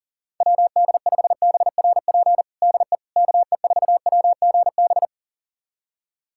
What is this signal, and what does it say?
Morse code, heard as a keyed tone, about 40 wpm, 700 Hz. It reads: WD5BRP DE KE4WGB
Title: USA Callsign DE Callsign 40wpm